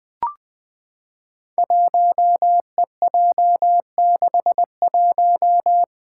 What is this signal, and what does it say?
Morse code, heard as a keyed tone, about 20 wpm, 700 Hz. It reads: 1EJ61
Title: Five Challenging Letter-Number Combinations 20wpm